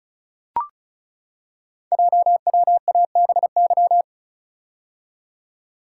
Morse code, heard as a keyed tone, about 36 wpm, 690 Hz. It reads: JWABY
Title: Code Groups - 5 Letters - 35wpm